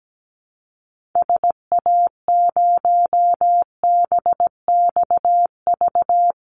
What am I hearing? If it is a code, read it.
Morse code, heard as a keyed tone, about 17 wpm, 700 Hz. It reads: SA0BXV
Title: Call Signs Repeated 3x - 17wpm